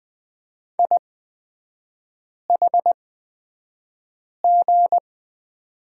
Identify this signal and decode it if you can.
Morse code, heard as a keyed tone, about 20 wpm, spaced as for 5 wpm, 700 Hz. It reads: IHG
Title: 243 Lesson 39e - Letter z - Three Characters 20wpm